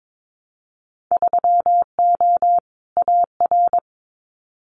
Morse code, heard as a keyed tone, about 22 wpm, 700 Hz. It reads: 3O AR
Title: Winter Field Day Exchanges Encore 22wpm